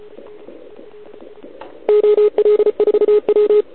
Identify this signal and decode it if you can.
Morse code, heard as a keyed tone, about 32 wpm, 410 Hz. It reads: OL4W
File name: ARCHIV / OKDX2020 / OL4Wa